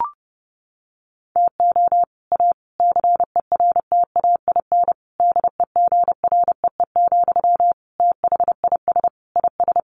Morse code, heard as a keyed tone, about 30 wpm, 700 Hz. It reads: TO A CERTAIN DEGREE, THIS IS
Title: Sentences from Top 1500 Words 30wpm